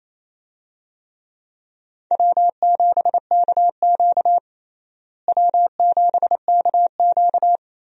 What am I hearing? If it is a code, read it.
Morse code, heard as a keyed tone, about 28 wpm, 700 Hz. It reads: W7KQ W7KQ
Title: USA Call Signs - 4 Characters - Repeated 3x - 28wpm